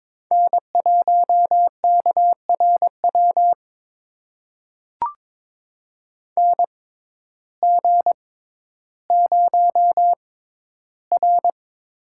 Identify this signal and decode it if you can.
Morse code, heard as a keyed tone, about 22 wpm, spaced as for 12 wpm, 700 Hz. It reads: N1KRW NG0R
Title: USA Call Signs - 6x Character Spacing - 22wpm